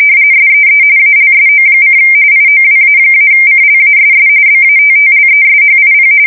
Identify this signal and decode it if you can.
RTTY, about 45.45 baud. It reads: SEE ALSO: IOWA-CLASS BATTLESHIP
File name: fsk170 / USS-Missouri